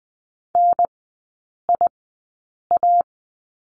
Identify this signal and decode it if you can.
Morse code, heard as a keyed tone, about 20 wpm, spaced as for 8 wpm, 700 Hz. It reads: NIA
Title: 218 Lesson 35e - Number 0 - Three Characters 20wpm